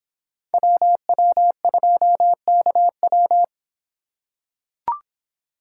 Morse code, heard as a keyed tone, about 24 wpm, 700 Hz. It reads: WW2KW